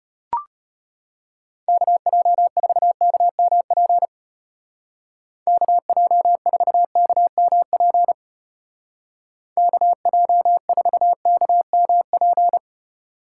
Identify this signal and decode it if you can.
Morse code, heard as a keyed tone, about 31 wpm, 700 Hz. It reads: KJ4KMP KJ4KMP KJ4KMP
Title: Speed Racing - USA 2x3 Call Signs 25wpm